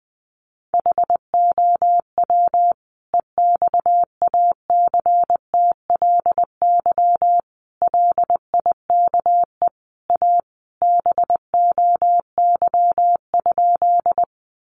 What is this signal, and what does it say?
Morse code, heard as a keyed tone, about 20 wpm, 700 Hz. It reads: HOW EXACTLY LIKE A BOY?
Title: Sentences from Top 900 Words 20wpm